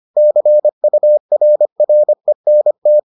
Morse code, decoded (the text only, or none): CURRENT